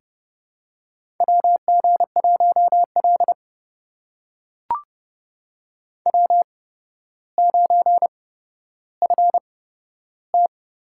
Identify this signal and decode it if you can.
Morse code, heard as a keyed tone, about 30 wpm, spaced as for 13 wpm, 700 Hz. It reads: WG1L W9FT
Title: USA Call Signs - 4 Characters - 8x Character Spacing - 30wpm